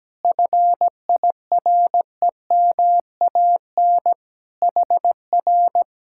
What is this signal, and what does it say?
Morse code, heard as a keyed tone, about 17 wpm, 700 Hz. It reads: FIREMAN HR
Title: Speed-Racing Common QSO Phrases - 17wpm